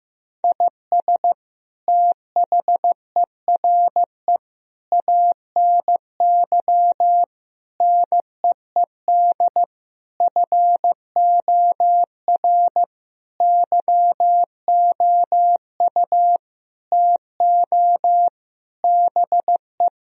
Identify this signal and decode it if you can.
Morse code, heard as a keyed tone, about 15 wpm, 700 Hz. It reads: IS THERE ANY NEED FOR YOU TO BE
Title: Sentences from Top 1000 Words - Review Part 4 - 15wpm